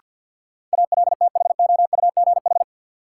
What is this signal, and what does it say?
Morse code, heard as a keyed tone, about 50 wpm, 700 Hz. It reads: ALTHOUGH
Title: Sentences from Top 600 Words - New Word Drill - 50wpm